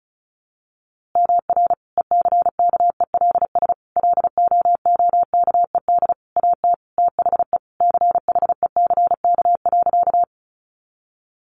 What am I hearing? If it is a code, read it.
Morse code, heard as a keyed tone, about 34 wpm, 700 Hz. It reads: MR ECKELS LOOKED AT THE CHECK.
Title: A Sound Of Thunder 35wpm